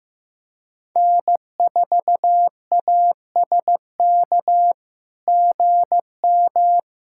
Morse code, heard as a keyed tone, about 16 wpm, 700 Hz. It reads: N4ASK GM
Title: SOTA QSO Practice 15wpm